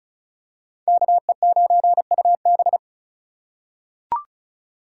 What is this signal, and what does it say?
Morse code, heard as a keyed tone, about 34 wpm, 700 Hz. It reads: KE9UB
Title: USA Call Signs - 5 Characters - 4x Character Spacing - 35wpm